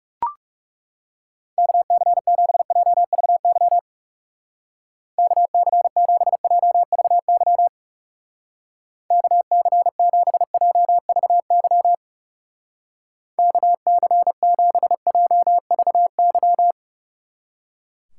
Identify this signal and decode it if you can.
Morse code, heard as a keyed tone, about 36 wpm, 700 Hz. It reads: KC7JVY KC7JVY KC7JVY KC7JVY
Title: Speed Racing - USA 2x3 Call Signs 30wpm